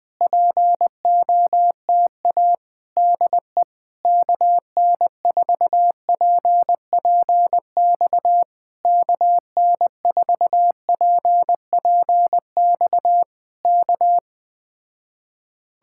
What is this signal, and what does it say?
Morse code, heard as a keyed tone, about 20 wpm, 700 Hz. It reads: POTA DE KN4PPX KN4PPX K